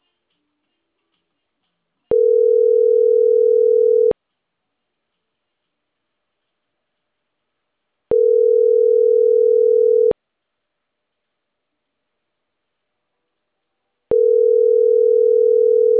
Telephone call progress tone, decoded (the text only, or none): ringback tone